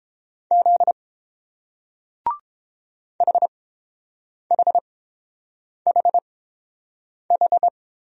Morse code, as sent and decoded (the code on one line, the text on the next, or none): --.. / .... .... .... ....
Z HHHH